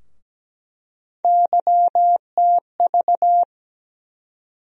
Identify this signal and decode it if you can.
Morse code, heard as a keyed tone, about 17 wpm, 700 Hz. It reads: YTV
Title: USA Call Sign Postfixes - 3 Letters 17wpm